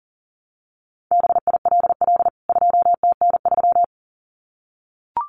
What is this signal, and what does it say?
Morse code, heard as a keyed tone, about 40 wpm, 700 Hz. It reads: BILL 2TN3
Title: CWT Exchange 40wpm